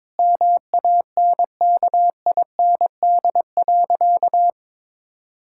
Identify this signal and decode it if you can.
Morse code, heard as a keyed tone, about 22 wpm, 700 Hz. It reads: MANKIND.
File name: Sentences with 9-Word Phrases - 22wpm